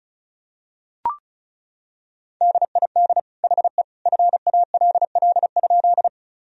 Morse code, heard as a keyed tone, about 34 wpm, 700 Hz. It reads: DID HE FALL?
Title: Sentences from Top 1000 Words - Review Part 2 - 35wpm